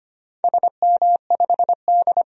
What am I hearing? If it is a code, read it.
Morse code, heard as a keyed tone, about 25 wpm, 700 Hz. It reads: SM5D
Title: DXCC-100 Call Signs - 4 Characters - 25wpm